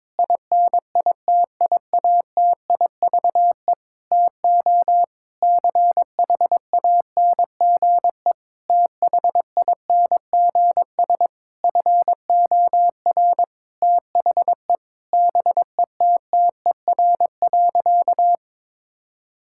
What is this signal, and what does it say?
Morse code, heard as a keyed tone, about 22 wpm, 700 Hz. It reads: INITIATIVE TO CHANGE THINGS FOR THE BETTER.
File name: Difficult Sentences with Hints - 22wpm